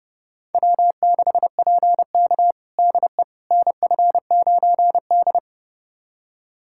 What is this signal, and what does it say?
Morse code, heard as a keyed tone, about 32 wpm, 700 Hz. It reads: W6PK DE NF9D
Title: USA Callsign DE Callsign - 4 Characters - 30wpm